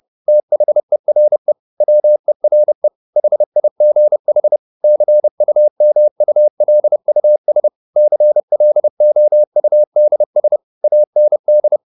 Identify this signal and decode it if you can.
Morse code, heard as a keyed tone, about 30 wpm, 600 Hz. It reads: THERE WERE HIGH CUMULUS CLOUDS AND